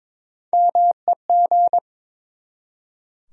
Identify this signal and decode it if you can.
Morse code, heard as a keyed tone, about 22 wpm, 700 Hz. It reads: MEG
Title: USA Common Names - Max 3 Letters - 22wpm